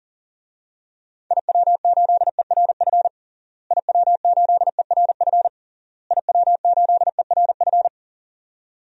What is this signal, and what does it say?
Morse code, heard as a keyed tone, about 37 wpm, 700 Hz. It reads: IW8ERF IW8ERF IW8ERF
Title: DXCC-100 Call Signs - 6 Characters - Repeated 3x - 40wpm